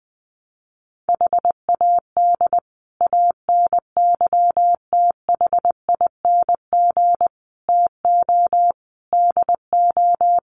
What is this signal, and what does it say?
Morse code, heard as a keyed tone, about 20 wpm, 700 Hz. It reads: HAD ANYTHING TO DO
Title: Sets of 5 Words Encore 20wpm